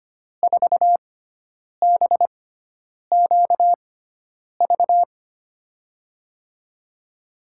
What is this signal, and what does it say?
Morse code, heard as a keyed tone, about 25 wpm, spaced as for 9 wpm, 700 Hz. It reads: 4BQV
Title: USA Call Signs - 6x Character Spacing - 25wpm